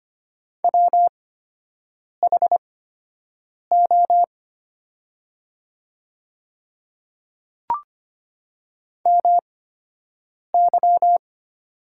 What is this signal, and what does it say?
Morse code, heard as a keyed tone, about 25 wpm, spaced as for 7 wpm, 700 Hz. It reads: WHO MY
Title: 8x Character Spacing - Top 100 Words - 25wpm